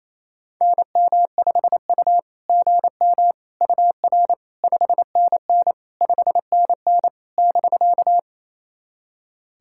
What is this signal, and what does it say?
Morse code, heard as a keyed tone, about 28 wpm, 690 Hz. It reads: NM5U GM UR 5NN 5NN <BK>
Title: SOTA QSO Practice 28wpm